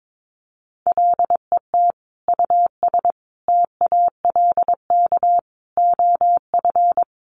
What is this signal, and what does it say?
Morse code, heard as a keyed tone, about 22 wpm, 700 Hz. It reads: LET US TALK OF
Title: Sentences from Top 800 Words 22wpm